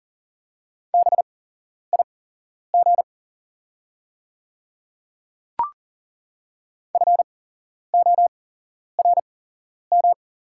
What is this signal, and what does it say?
Morse code, heard as a keyed tone, about 40 wpm, spaced as for 11 wpm, 690 Hz. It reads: BIG FORM
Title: 8x Character Spacing - Top 200 Words - 40wpm